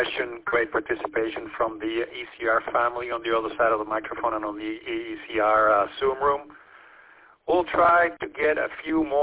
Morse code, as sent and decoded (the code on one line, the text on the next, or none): none